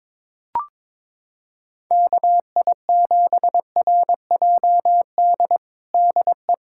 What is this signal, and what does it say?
Morse code, heard as a keyed tone, about 21 wpm, 700 Hz. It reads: KI7RJD DE
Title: USA Callsign DE Callsign - 6 Characters - 22wpm